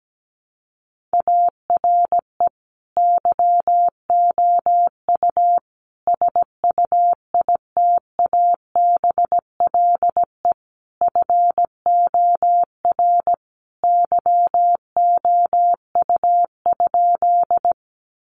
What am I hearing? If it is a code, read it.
Morse code, heard as a keyed tone, about 17 wpm, 700 Hz. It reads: ARE YOU SUITABLE FOR YOU?
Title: Sentences from Top 1400 Words 17wpm